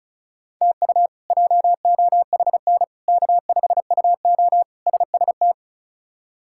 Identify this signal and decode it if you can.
Morse code, heard as a keyed tone, about 36 wpm, 700 Hz. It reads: TU JOHN K5UO SST
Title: K1USN SST QSO Practice - 35wpm